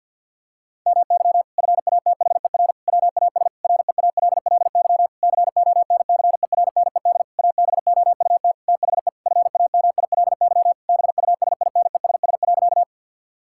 Morse code, heard as a keyed tone, about 53 wpm, 700 Hz. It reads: MY FATHER WAS REALLY CONCERNED ABOUT THE FAMILY BUSINESS.
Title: Sentences from Top 1000 Words - Review Part 6 - 50wpm